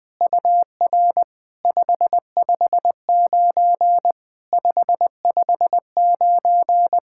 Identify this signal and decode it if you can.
Morse code, heard as a keyed tone, about 21 wpm, 700 Hz. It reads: UR 559 559
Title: POTA QSO Practice - 20wpm